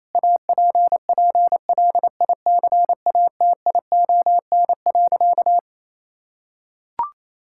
Morse code, heard as a keyed tone, about 28 wpm, 700 Hz. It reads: APPLICATION.
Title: Sentences from Top 1600 Words 28wpm